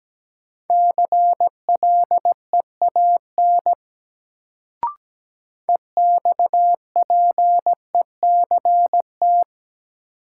Morse code, heard as a keyed tone, about 17 wpm, 700 Hz. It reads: CLEAN EXPECT
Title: Sentences from Top 1100 Words - New Word Drill Alt - 17wpm